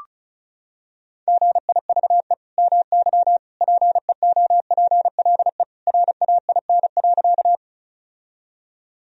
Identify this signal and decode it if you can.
Morse code, heard as a keyed tone, about 34 wpm, 700 Hz. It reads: GIVE MY PEOPLE RAIN.
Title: Sentences from Top 1000 Words - Review Part 1 - 35wpm